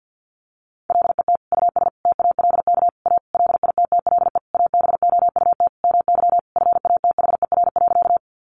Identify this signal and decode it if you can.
Morse code, heard as a keyed tone, about 50 wpm, 700 Hz. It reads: LET US TALK A LITTLE ABOUT MY FATHER.
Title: Sentences from Top 400 Words 50wpm